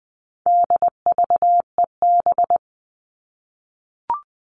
Morse code, heard as a keyed tone, about 20 wpm, 700 Hz. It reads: DVEB
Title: All Cut Numbers - 4-Digit 20wpm